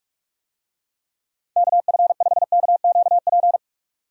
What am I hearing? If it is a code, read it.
Morse code, heard as a keyed tone, about 46 wpm, 700 Hz. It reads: KF5KQP
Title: USA Call Signs - 6 Characters - 45wpm